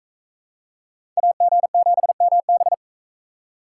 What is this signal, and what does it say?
Morse code, heard as a keyed tone, about 43 wpm, 700 Hz. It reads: AG7MB